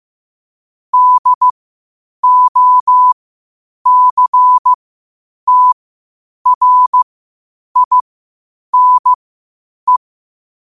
Morse code, heard as a keyed tone, about 15 wpm, spaced as for 8 wpm, 1000 Hz. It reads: DOCTRINE
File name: qotdfiles / 8wpm